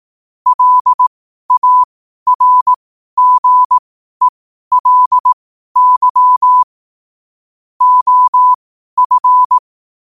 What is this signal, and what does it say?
Morse code, heard as a keyed tone, about 18 wpm, spaced as for 12 wpm, 1000 Hz. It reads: LARGELY OF
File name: qotdfiles / 12wpm